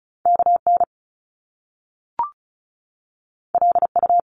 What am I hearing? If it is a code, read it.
Morse code, heard as a keyed tone, about 36 wpm, 700 Hz. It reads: KN LU
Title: Top Two Letter Combinations in Words 35wpm